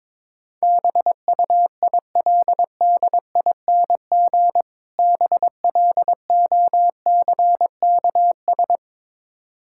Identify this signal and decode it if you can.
Morse code, heard as a keyed tone, about 22 wpm, 700 Hz. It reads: BUILDING BLOCKS
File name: Difficult Words with Hints - 22wpm